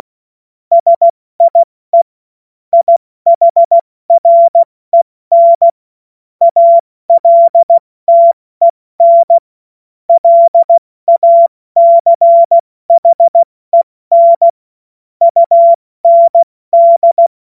Morse code, heard as a keyed tone, about 16 wpm, 680 Hz. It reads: SIE IHREN ALTEN LACHEN UND
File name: Morsekurs / Witze / Witz 09 April 2024 70 F80